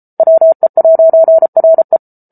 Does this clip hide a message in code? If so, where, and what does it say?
Yes: Morse code from 0.2 to 2.0 s, heard as a keyed tone, about 34 wpm, 650 Hz. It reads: WE'RE